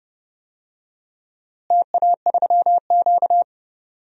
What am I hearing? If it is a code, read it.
Morse code, heard as a keyed tone, about 32 wpm, 700 Hz. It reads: TA3Q
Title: DXCC-100 Call Signs - 4 Characters - Repeated 3x - 30wpm